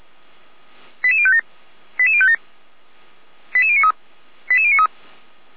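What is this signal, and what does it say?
Five-tone selective call (ZVEI), heard as quick runs of five tones, tones of about 70 ms. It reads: 88057 88057 88093 88093
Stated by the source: ZVEI five-tone selective call received by SDR, Feuerwehr Orlamünde fire brigade alerting.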